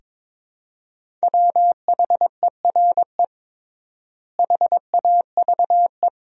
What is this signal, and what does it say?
Morse code, heard as a keyed tone, about 22 wpm, 700 Hz. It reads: WHERE HAVE
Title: 3x Word-Spacing - Sentences from Top 200 Words - 22wpm